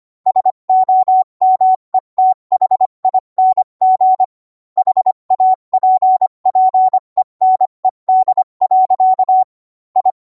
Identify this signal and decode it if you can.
Morse code, heard as a keyed tone, about 25 wpm, 750 Hz. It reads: SOMETHING HAPPENED. I